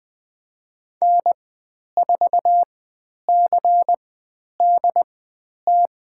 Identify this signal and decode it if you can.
Morse code, heard as a keyed tone, about 20 wpm, spaced as for 10 wpm, 700 Hz. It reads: N4CDT
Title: 095 Lesson 16g - Letter c - Call Signs 20wpm